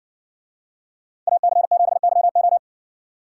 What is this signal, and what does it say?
Morse code, heard as a keyed tone, about 58 wpm, 700 Hz. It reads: AC7YO